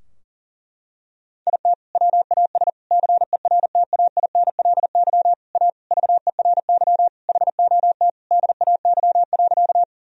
Morse code, heard as a keyed tone, about 37 wpm, 700 Hz. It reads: IT WAS CERTAINLY A VERY HOT DAY.
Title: Sentences from Top 1000 Words - Review Part 3 - 40wpm